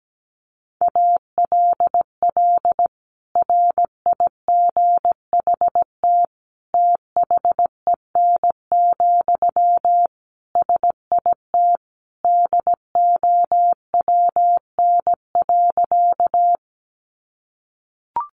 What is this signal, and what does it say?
Morse code, heard as a keyed tone, about 17 wpm, 700 Hz. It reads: ALL RIGHT THEN, SIT DOWN.
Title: Sentences from Top 500 Words 17wpm